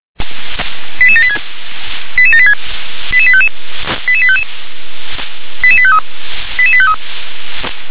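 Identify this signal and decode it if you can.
Five-tone selective call (ZVEI), heard as quick runs of five tones, tones of about 70 ms. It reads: 88786 88786 88855 88855 88853 88853
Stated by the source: ZVEI five-tone selective call 88747 received on SDR.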